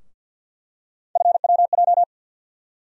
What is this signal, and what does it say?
Morse code, heard as a keyed tone, about 52 wpm, 700 Hz. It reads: FWJ
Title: USA Call Sign Postfixes - 3 Letters 50wpm